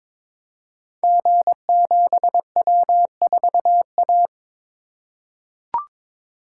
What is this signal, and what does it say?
Morse code, heard as a keyed tone, about 22 wpm, 690 Hz. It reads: G7W4A
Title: Code Groups - 5 Letter:Numbers - 22wpm